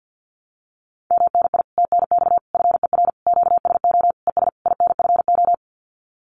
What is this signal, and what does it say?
Morse code, heard as a keyed tone, about 50 wpm, 700 Hz. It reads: MNI TNX FER QSO ES INFO
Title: Common QSO Phrases Encore 50wpm